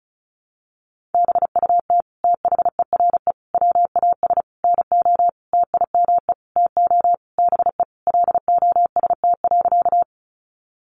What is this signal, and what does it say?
Morse code, heard as a keyed tone, about 34 wpm, 700 Hz. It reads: BUT THERE WAS NO TIME TO BE LOST.
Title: Sentences from Top 1400 Words 35wpm